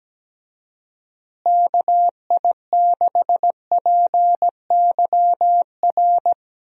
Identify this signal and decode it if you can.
Morse code, heard as a keyed tone, about 17 wpm, 700 Hz. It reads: KI6PYR
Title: USA Call Signs - 6 Characters - Repeated 2x - 17wpm